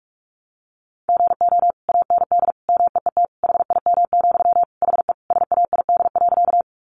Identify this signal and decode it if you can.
Morse code, heard as a keyed tone, about 44 wpm, 700 Hz. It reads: GO AND MEET HIM, HE SAID.
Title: Sentences from Top 1100 Words 45wpm